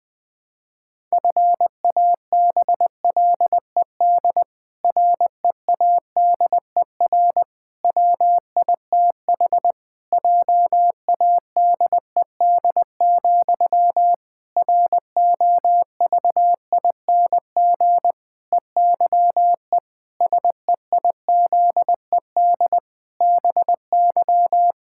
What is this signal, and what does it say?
Morse code, heard as a keyed tone, about 21 wpm, 700 Hz. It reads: FABLED READER WITH JADED, ROVING EYE SEIZED BY
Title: Pangram Challenge v2 - 20wpm